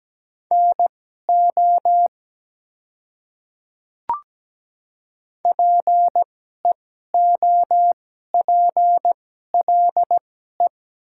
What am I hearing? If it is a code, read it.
Morse code, heard as a keyed tone, about 17 wpm, spaced as for 12 wpm, 700 Hz. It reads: NO PEOPLE
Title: 2x Character Spacing - Top 100 Words - 17wpm